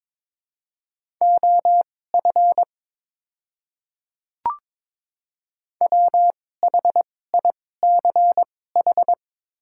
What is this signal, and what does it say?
Morse code, heard as a keyed tone, about 22 wpm, spaced as for 16 wpm, 700 Hz. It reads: OF WHICH
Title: 2x Character Spacing - Top 100 Words - 22wpm